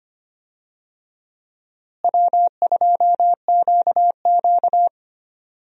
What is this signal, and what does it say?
Morse code, heard as a keyed tone, about 25 wpm, 700 Hz. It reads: W2QQ
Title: USA Call Signs - 4 Characters - Repeated 3x - 25wpm